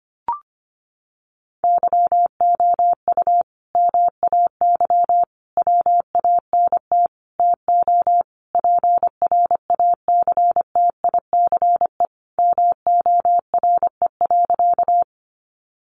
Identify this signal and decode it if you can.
Morse code, heard as a keyed tone, about 25 wpm, 700 Hz. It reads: YOU MAY WANT TO PRACTICE MORE.
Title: Sentences from Top 700 Words 25wpm